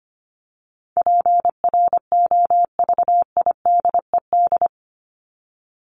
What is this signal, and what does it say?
Morse code, heard as a keyed tone, about 25 wpm, 700 Hz. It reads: PROVIDED
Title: Top 1751-2000 Words 25wpm